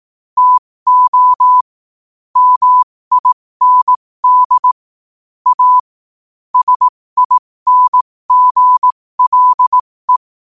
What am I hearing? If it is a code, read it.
Morse code, heard as a keyed tone, about 18 wpm, spaced as for 15 wpm, 1000 Hz. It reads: TO MIND A SINGLE
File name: 15wpm